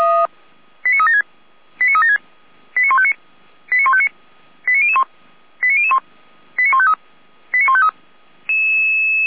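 Five-tone selective call (ZVEI), heard as quick runs of five tones, tones of about 70 ms. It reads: 89376 89376 89159 89159 89001 89001 89153 89153
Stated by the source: ZVEI five-tone call 89160 received by SDR, from the Feuerwehr Orlamünde fire brigade audio archive.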